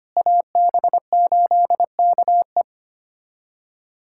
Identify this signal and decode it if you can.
Morse code, heard as a keyed tone, about 25 wpm, 700 Hz. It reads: AB8KE
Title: USA Callsign DE Callsign - 5 Characters - 25wpm